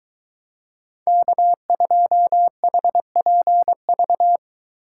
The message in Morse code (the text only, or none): K2HPV